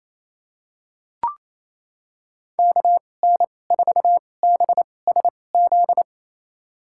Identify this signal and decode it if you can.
Morse code, heard as a keyed tone, about 28 wpm, spaced as for 20 wpm, 700 Hz. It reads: KN4BSZ
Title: USA Call Signs - 6 Characters - 2x Character Spacing - 28wpm